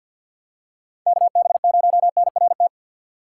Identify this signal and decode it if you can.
Morse code, heard as a keyed tone, about 52 wpm, 700 Hz. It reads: KB0NRT
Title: USA Call Signs - 6 Characters - 4x Character Spacing - 50wpm